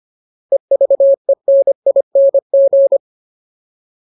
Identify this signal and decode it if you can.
Morse code, heard as a keyed tone, about 25 wpm, 550 Hz. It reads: EVENING
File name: Ultimate Word List / 25WPM Ultimate